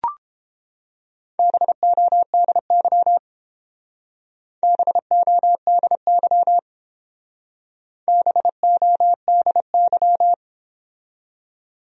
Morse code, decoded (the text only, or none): BODY BODY BODY